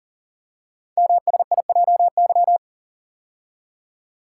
Morse code, heard as a keyed tone, about 40 wpm, 690 Hz. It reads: MSIJY
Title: Code Groups - 5 Letters - 40wpm